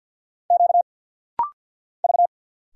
Morse code, heard as a keyed tone, about 50 wpm, 700 Hz. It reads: Q V
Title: Single Letters - Rapid-Fire 50wpm